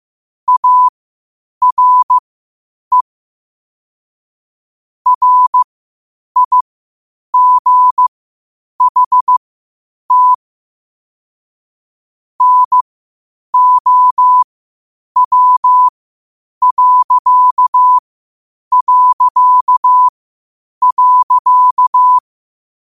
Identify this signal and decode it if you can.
Morse code, heard as a keyed tone, about 15 wpm, spaced as for 8 wpm, 1000 Hz. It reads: ARE RIGHT NOW...
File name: qotdfiles / 8wpm